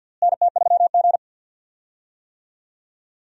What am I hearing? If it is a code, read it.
Morse code, heard as a keyed tone, about 52 wpm, 700 Hz. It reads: NT3G
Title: USA Call Signs - 4 Characters - Repeated 3x - 50wpm